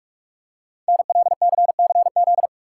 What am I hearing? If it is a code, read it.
Morse code, heard as a keyed tone, about 44 wpm, 690 Hz. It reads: NLCCZ
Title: Code Groups - 5 Letters - 45wpm